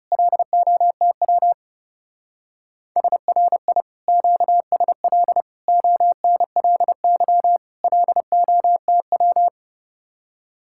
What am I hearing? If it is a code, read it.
Morse code, heard as a keyed tone, about 37 wpm, 700 Hz. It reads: LOTW SRI QSL ONLY LOTW
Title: Speed-Racing Common QSO Phrases - 30wpm